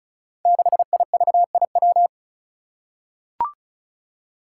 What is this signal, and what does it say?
Morse code, heard as a keyed tone, about 36 wpm, 700 Hz. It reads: 6IVIW
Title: Five Challenging Letter-Number Combinations 35wpm